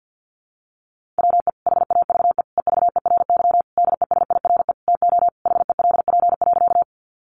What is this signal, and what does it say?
Morse code, heard as a keyed tone, about 49 wpm, 700 Hz. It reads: WE HAVE EVERY DESIRE TO HELP.
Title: Sentences from Top 1100 Words 50wpm